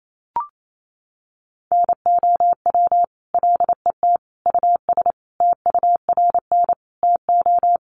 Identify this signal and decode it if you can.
Morse code, heard as a keyed tone, about 28 wpm, 700 Hz. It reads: NOW LET US TURN TO
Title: Sentences from Top 600 Words 28wpm